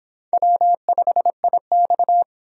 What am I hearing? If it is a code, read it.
Morse code, heard as a keyed tone, about 25 wpm, 700 Hz. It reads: W5IX